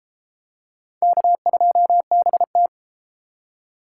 Morse code, heard as a keyed tone, about 28 wpm, 700 Hz. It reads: K2BT